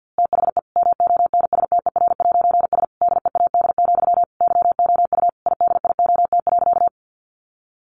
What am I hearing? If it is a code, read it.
Morse code, heard as a keyed tone, about 50 wpm, 700 Hz. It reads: THE MONSTER'S DEAD, YOU IDIOT.
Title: A Sound Of Thunder 50wpm